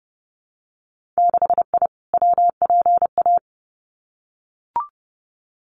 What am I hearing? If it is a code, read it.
Morse code, heard as a keyed tone, about 30 wpm, 700 Hz. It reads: 6I WPA
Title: Winter Field Day Exchanges Encore 30wpm